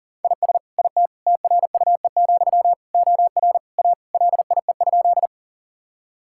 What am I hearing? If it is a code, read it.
Morse code, heard as a keyed tone, about 40 wpm, 700 Hz. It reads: IS IT TRUE, OR A LIE?
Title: Sentences from Top 1600 Words 40wpm